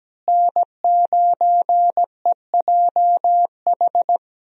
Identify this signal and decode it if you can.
Morse code, heard as a keyed tone, about 17 wpm, 700 Hz. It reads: N9EJH
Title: USA Call Signs - 5 Characters - Repeated 3x - 17wpm